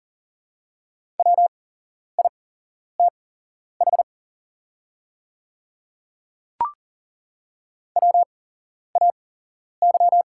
Morse code, heard as a keyed tone, about 40 wpm, spaced as for 11 wpm, 690 Hz. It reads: WITH WAY
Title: 8x Character Spacing - Top 200 Words - 40wpm